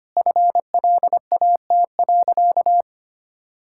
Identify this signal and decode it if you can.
Morse code, heard as a keyed tone, about 25 wpm, 690 Hz. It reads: FLAT.
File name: Easy Sentences with Hints - 25wpm